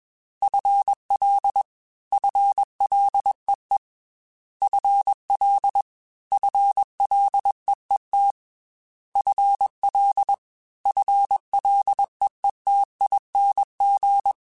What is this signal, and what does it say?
Morse code, heard as a keyed tone, about 22 wpm, 800 Hz. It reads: FL FLEE FL FLEET FL FLEETING